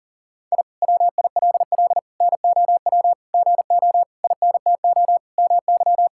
Morse code, heard as a keyed tone, about 40 wpm, 700 Hz. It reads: I WILL NOW GO INTO MY
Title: Sentences from Top 1000 Words - Review Part 6 - 40wpm